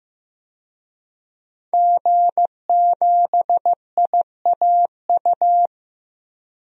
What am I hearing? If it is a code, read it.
Morse code, heard as a keyed tone, about 15 wpm, 700 Hz. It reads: G7IAU
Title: DXCC-100 Call Signs - 5 Characters - 15wpm